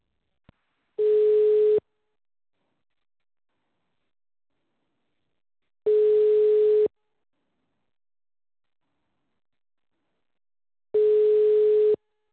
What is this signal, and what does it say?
Ringback tone (Europe, 425 Hz).